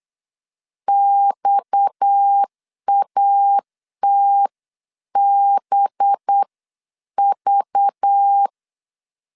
Morse code, heard as a keyed tone, about 8 wpm, spaced as for 7 wpm, 800 Hz. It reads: XATBV
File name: CW / DET / 1 6 1500C S B 8wpm 5142